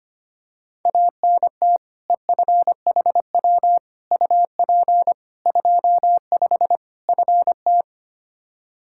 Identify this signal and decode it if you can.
Morse code, heard as a keyed tone, about 25 wpm, 700 Hz. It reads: ANT EFHW UP 25 FT
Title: K1USN SST QSO Practice 25wpm